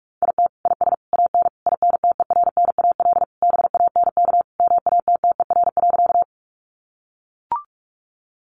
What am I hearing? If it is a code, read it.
Morse code, heard as a keyed tone, about 44 wpm, 700 Hz. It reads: IT IS AN INTERNAL BANK MATTER.
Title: Sentences from Top 1400 Words 45wpm